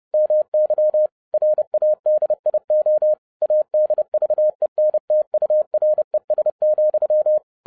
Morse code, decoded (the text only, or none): MY RADIO ADVENTURES,